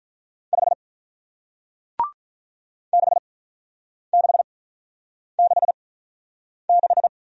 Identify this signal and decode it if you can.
Morse code, heard as a keyed tone, about 47 wpm, spaced as for 9 wpm, 700 Hz. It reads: 5 6666